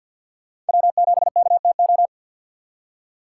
Morse code, heard as a keyed tone, about 53 wpm, 700 Hz. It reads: W7KTO
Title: USA Call Signs - 5 Characters - 6x Character Spacing - 50wpm